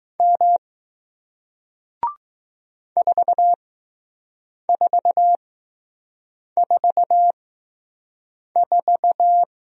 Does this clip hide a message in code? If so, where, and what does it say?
Yes: Morse code from 0.2 to 9.4 s, heard as a keyed tone, about 20 wpm, spaced as for 6 wpm, 700 Hz. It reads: M 4444